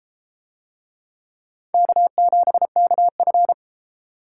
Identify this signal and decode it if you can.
Morse code, heard as a keyed tone, about 31 wpm, 700 Hz. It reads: K7KF